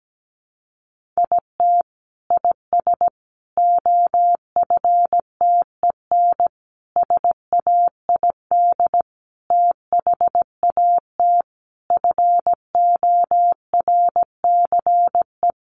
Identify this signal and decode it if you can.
Morse code, heard as a keyed tone, about 17 wpm, 700 Hz. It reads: IT IS OFTEN SAID THAT FORCE
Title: Sentences from Top 1400 Words 17wpm